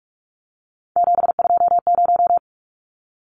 Morse code, heard as a keyed tone, about 44 wpm, 700 Hz. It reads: 720
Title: Numbers - 3-Digit 45wpm